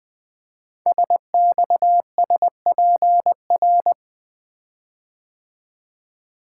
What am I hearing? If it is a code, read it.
Morse code, heard as a keyed tone, about 20 wpm, 700 Hz. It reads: SXSPR
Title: Code Groups - 5 Letters - 20wpm